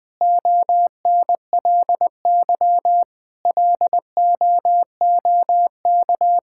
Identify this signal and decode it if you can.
Morse code, heard as a keyed tone, about 20 wpm, 700 Hz. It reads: ONLY LOOK
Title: Sentences from Top 1600 Words 20wpm